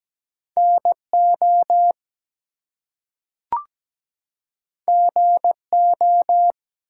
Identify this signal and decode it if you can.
Morse code, heard as a keyed tone, about 17 wpm, 700 Hz. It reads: NO GO
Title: Two Letter Words - 17wpm